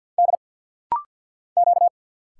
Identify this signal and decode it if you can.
Morse code, heard as a keyed tone, about 50 wpm, 700 Hz. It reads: D Q
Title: Single Letter-Number - Rapid-Fire 50wpm